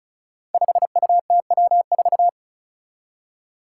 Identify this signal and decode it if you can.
Morse code, heard as a keyed tone, about 36 wpm, 700 Hz. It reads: 5UTW4
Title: Five Challenging Letter-Number Combinations 35wpm